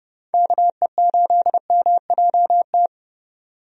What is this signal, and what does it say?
Morse code, heard as a keyed tone, about 30 wpm, 700 Hz. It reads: KE8MJT
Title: USA Call Signs - 6 Characters - 30wpm